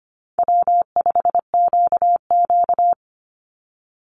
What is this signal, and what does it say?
Morse code, heard as a keyed tone, about 25 wpm, 700 Hz. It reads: W5QQ
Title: Easy Call Signs - 25wpm